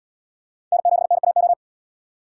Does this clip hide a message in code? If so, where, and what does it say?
Yes: Morse code from 0.7 to 1.5 s, heard as a keyed tone, about 77 wpm, 700 Hz. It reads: N7NAO